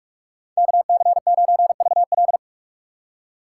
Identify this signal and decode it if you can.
Morse code, heard as a keyed tone, about 45 wpm, 700 Hz. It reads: KC9VL
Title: USA Call Signs - 5 Characters - 2x Character Spacing - 45wpm